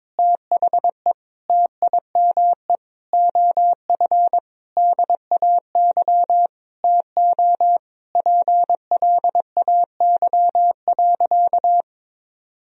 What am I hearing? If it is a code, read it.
Morse code, heard as a keyed tone, about 22 wpm, 700 Hz. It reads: THE TIME OF DAY TO PLAY.
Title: Sentences from Top 1000 Words - Review Part 1 - 22wpm